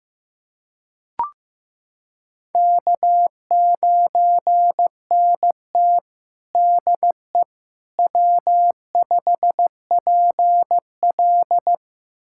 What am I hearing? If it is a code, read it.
Morse code, heard as a keyed tone, about 16 wpm, 700 Hz. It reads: K9NT DE W5PL
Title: USA Callsign DE Callsign - 4 Characters - 15wpm